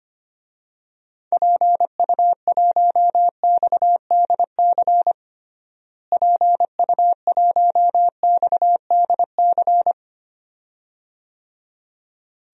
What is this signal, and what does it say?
Morse code, heard as a keyed tone, about 25 wpm, 700 Hz. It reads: PU1XDC PU1XDC
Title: DXCC-100 Call Signs - 6 Characters - Repeated 2x - 25wpm